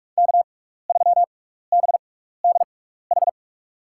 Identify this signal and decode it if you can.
Morse code, heard as a keyed tone, about 44 wpm, spaced as for 15 wpm, 700 Hz. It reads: K3BDH
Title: USA Call Signs - 5 Characters - 6x Character Spacing - 45wpm